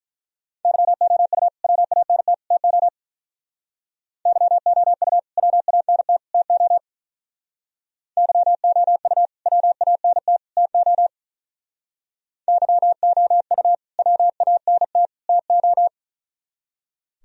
Morse code, heard as a keyed tone, about 43 wpm, 700 Hz. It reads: YOU WANT TO YOU WANT TO YOU WANT TO YOU WANT TO
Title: Speed-Racing - Sets of 3 Words 35wpm